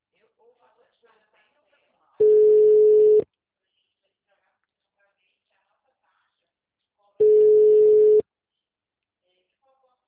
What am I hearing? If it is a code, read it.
Ringback tone (Europe, 425 Hz), with speech.
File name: out- -2019- - - .65302